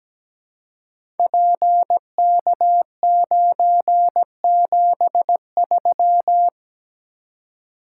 Morse code, heard as a keyed tone, about 17 wpm, 700 Hz. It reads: PK973
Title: Code Groups - 5 Letter:Numbers - 17wpm